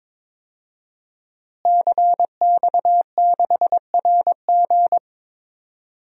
Morse code, heard as a keyed tone, about 21 wpm, 700 Hz. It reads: CX6RG
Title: DXCC-100 Call Signs - 5 Characters - Repeated 2x - 22wpm